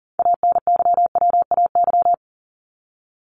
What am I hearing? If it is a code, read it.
Morse code, heard as a keyed tone, about 40 wpm, 700 Hz. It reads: ANYWAY
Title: Top 1501-1750 Words 40wpm